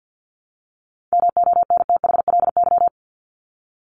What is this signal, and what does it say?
Morse code, heard as a keyed tone, about 48 wpm, 700 Hz. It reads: MONTHLY
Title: LY Suffix - Top 100 Words with Base Words 50wpm